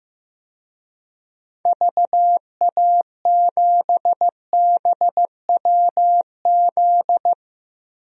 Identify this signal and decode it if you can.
Morse code, heard as a keyed tone, about 15 wpm, 700 Hz. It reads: VA7BWZ
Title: DXCC-100 Call Signs - 6 Characters - Repeated 2x - 15wpm